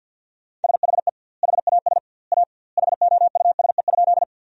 Morse code, heard as a keyed tone, about 50 wpm, 700 Hz. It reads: SHE HAS A HOUSE?
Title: 2x Word-Spacing - Sentences from Top 200 Words - 50wpm